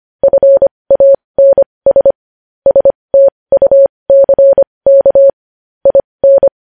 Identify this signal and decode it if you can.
Morse code, heard as a keyed tone, about 25 wpm, spaced as for 20 wpm, 550 Hz. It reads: FANS STUCK IN
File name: podcast / News headlines-1x-25x20